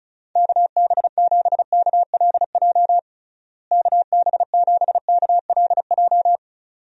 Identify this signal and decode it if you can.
Morse code, heard as a keyed tone, about 36 wpm, 700 Hz. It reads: KB7KLJ KB7KLJ
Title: USA Call Signs - 6 Characters - Repeated 2x - 35wpm